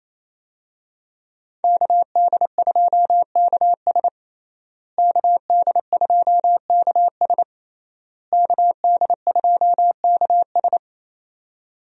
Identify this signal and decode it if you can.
Morse code, heard as a keyed tone, about 28 wpm, 700 Hz. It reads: KD2KS KD2KS KD2KS
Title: USA Call Signs - 5 Characters - Repeated 3x - 28wpm